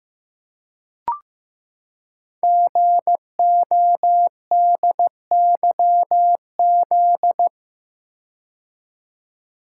Morse code, heard as a keyed tone, about 15 wpm, 690 Hz. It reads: GODYZ
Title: Code Groups - 5 Letters - 15wpm